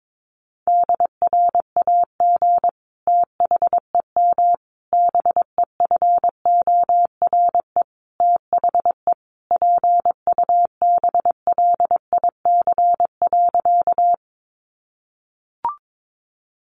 Morse code, heard as a keyed tone, about 22 wpm, 700 Hz. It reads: DRAG THEM BEFORE THE PUBLIC.
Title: Sentences from Top 1500 Words 22wpm